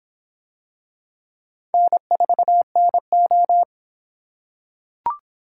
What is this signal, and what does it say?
Morse code, heard as a keyed tone, about 25 wpm, 700 Hz. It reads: N4NO